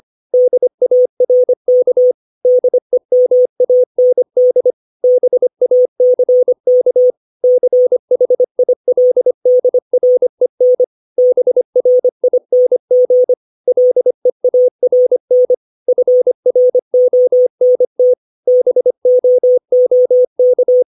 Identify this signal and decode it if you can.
Morse code, heard as a keyed tone, about 25 wpm, 500 Hz. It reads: DARK DEMAND BACK CHILDREN BRING LEARN FRONT BOOK
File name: 632 words / 632 Words 25WPM